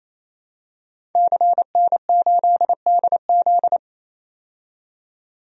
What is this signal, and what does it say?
Morse code, heard as a keyed tone, about 28 wpm, 700 Hz. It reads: CN8DZ
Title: DXCC-100 Call Signs - 5 Characters - 28wpm